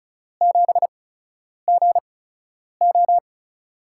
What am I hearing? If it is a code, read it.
Morse code, heard as a keyed tone, about 36 wpm, 700 Hz. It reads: 7 G O
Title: USA Call Signs - 4 Characters - 8x Character Spacing - 35wpm